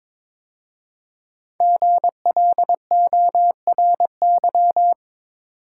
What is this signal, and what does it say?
Morse code, heard as a keyed tone, about 22 wpm, 700 Hz. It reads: GLORY
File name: 5 Letter Words - First 3 Letters Given - 22wpm